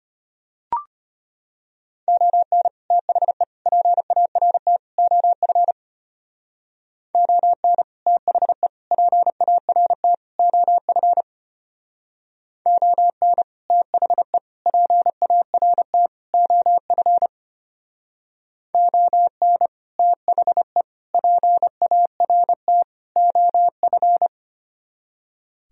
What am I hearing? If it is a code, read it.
Morse code, heard as a keyed tone, about 30 wpm, 700 Hz. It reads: ON THE PART OF ON THE PART OF ON THE PART OF ON THE PART OF
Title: Speed-Racing - Sets of 4 Words 25wpm